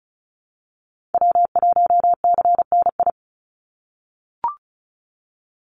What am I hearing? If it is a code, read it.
Morse code, heard as a keyed tone, about 34 wpm, 700 Hz. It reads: W1CNI
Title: USA Call Signs 35wpm